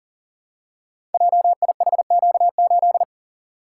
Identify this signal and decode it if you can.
Morse code, heard as a keyed tone, about 40 wpm, 700 Hz. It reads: JIHQ8
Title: Code Groups - 5 Letter:Numbers - 40wpm